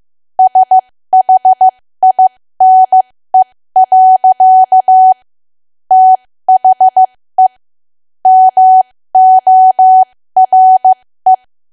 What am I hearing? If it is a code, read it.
Morse code, heard as a keyed tone, about 15 wpm, spaced as for 13 wpm, 750 Hz. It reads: SHINE. THE MORE
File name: cwa / W1AW-Level2 / 13WPM